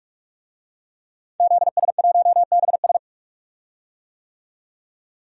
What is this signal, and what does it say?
Morse code, heard as a keyed tone, about 46 wpm, 700 Hz. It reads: ZS1BS
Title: DXCC-100 Call Signs - 5 Characters - 45wpm